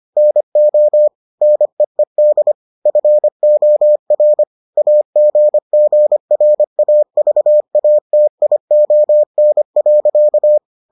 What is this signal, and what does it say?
Morse code, heard as a keyed tone, about 25 wpm, 600 Hz. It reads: NO NEED FOR AGGRAVATION.